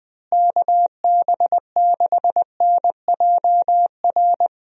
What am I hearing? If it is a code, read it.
Morse code, heard as a keyed tone, about 19 wpm, 700 Hz. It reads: KB6NJR
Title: USA Call Signs - 6 Characters - Repeated 3x - 20wpm